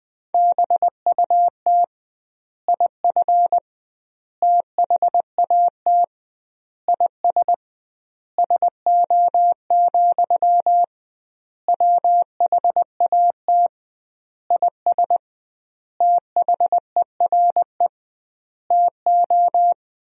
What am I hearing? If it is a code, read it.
Morse code, heard as a keyed tone, about 20 wpm, 700 Hz. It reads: BUT IF THAT IS SO, WHAT IS THERE TO
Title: 2x Word-Spacing - Sentences from Top 200 Words - 20wpm